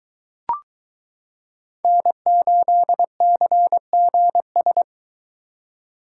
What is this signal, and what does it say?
Morse code, heard as a keyed tone, about 22 wpm, 700 Hz. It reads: N8CGS